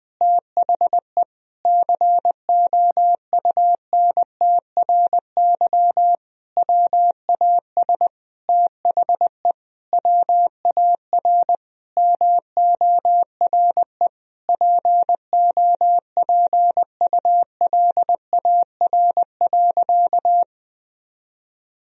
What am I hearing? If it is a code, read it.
Morse code, heard as a keyed tone, about 21 wpm, 700 Hz. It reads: THE COUNTRY WAS THE WAR MORE POPULAR.
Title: Sentences from Top 1000 Words - Review Part 3 - 20wpm